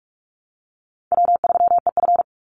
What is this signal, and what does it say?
Morse code, heard as a keyed tone, about 46 wpm, 700 Hz. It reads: W3EF
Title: Easy Call Signs - 45wpm